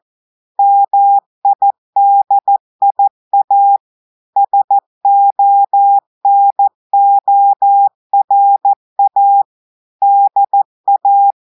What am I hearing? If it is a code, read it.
Morse code, heard as a keyed tone, about 14 wpm, 800 Hz. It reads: MIDIA SONORA DA
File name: audios / portugais / portugais texte en clair 1 vitesse 14 mots minutes